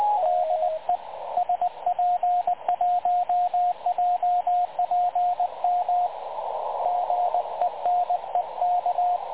Morse code, heard as a keyed tone, about 15 wpm, 750 Hz, with radio noise. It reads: DE SP1JPM GENEK
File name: dlazaawansowanych / nagraniazpasma / websdr recording 3565.0kHz